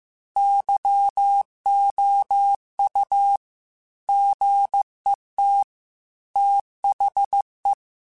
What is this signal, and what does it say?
Morse code, heard as a keyed tone, about 15 wpm, 800 Hz. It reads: YOU GET THE
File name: Technical Subjects / Technical Subjects-mp3s / AND Gate @13